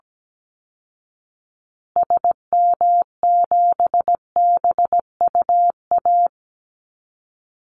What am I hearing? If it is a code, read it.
Morse code, heard as a keyed tone, about 17 wpm, 700 Hz. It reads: SM7BUA
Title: Call Signs Repeated 2x - 17wpm